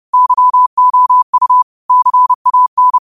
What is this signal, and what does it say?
Morse code, heard as a keyed tone, about 30 wpm, 1000 Hz. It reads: YOU CAN
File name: qotdfiles / 30wpm